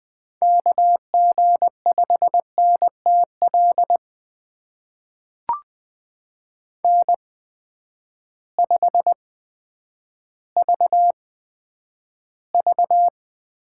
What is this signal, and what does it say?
Morse code, heard as a keyed tone, about 20 wpm, spaced as for 10 wpm, 700 Hz. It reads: KG5NTL N5VV
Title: USA Call Signs - 8x Character Spacing - 20wpm